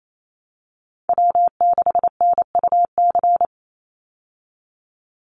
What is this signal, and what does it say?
Morse code, heard as a keyed tone, about 28 wpm, 700 Hz. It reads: W6NUC
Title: Call Signs Repeated 3x - 28wpm